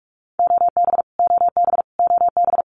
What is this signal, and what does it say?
Morse code, heard as a keyed tone, about 44 wpm, 700 Hz. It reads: OB OB OB
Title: Abbreviations Repeated 3x - 45wpm